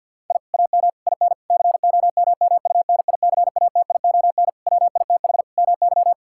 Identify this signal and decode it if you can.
Morse code, heard as a keyed tone, about 51 wpm, 700 Hz. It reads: I AM IN COMMUNICATION WITH MY
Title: Sentences from Top 1000 Words - Review Part 6 - 50wpm